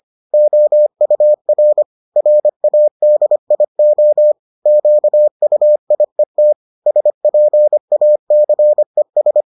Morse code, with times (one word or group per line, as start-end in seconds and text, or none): 0.3-1.8 OUR
2.2-4.3 RADIO
4.7-6.5 QUIET
6.9-9.4 SPACES